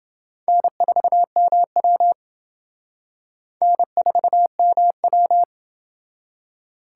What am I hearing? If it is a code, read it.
Morse code, heard as a keyed tone, about 27 wpm, 700 Hz. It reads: N4MW N4MW